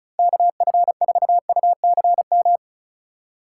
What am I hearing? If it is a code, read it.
Morse code, heard as a keyed tone, about 36 wpm, 700 Hz. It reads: KF4UCM
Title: USA Call Signs - 6 Characters - 2x Character Spacing - 35wpm